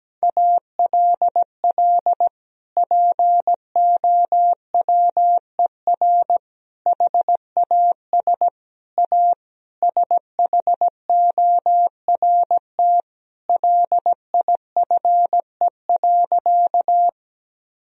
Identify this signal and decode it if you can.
Morse code, heard as a keyed tone, about 17 wpm, 700 Hz. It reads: ALL POWER HAS A SHORT LIFE.
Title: Sentences from Top 1000 Words - Review Part 2 - 17wpm